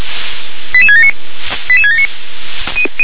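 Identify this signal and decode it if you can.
Five-tone selective call (ZVEI), heard as quick runs of five tones, tones of about 70 ms. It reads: 88679 88679
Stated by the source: ZVEI five-tone selective call 88671, received with an SDR.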